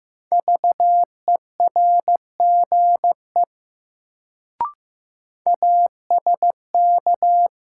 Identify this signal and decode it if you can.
Morse code, heard as a keyed tone, about 15 wpm, 700 Hz. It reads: VERGE ASK
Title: Sets of 4 Words - New Word Drill - 15wpm